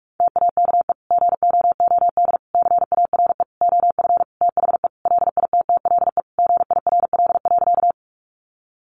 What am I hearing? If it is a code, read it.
Morse code, heard as a keyed tone, about 44 wpm, 700 Hz. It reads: TAKE GOOD CARE OF THE LITTLE GIRL.
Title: Sentences from Top 400 Words 45wpm